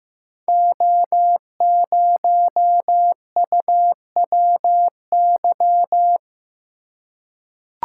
Morse code, heard as a keyed tone, about 15 wpm, 700 Hz. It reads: O0UWY
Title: Code Groups - 5 Letter:Numbers - 15wpm